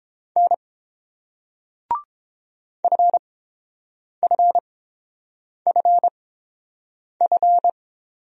Morse code, heard as a keyed tone, about 30 wpm, spaced as for 7 wpm, 690 Hz. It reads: N FFFF